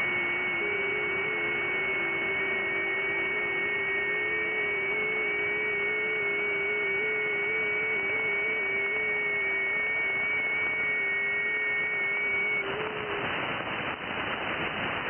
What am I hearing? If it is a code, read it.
FT8.
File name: IC-7100